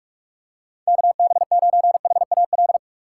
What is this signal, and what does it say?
Morse code, heard as a keyed tone, about 44 wpm, 700 Hz. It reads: KB9HAL